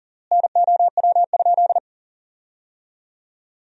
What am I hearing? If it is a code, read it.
Morse code, heard as a keyed tone, about 40 wpm, 700 Hz. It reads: NOW?
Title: 3x Word-Spacing - Sentences from Top 200 Words - 40wpm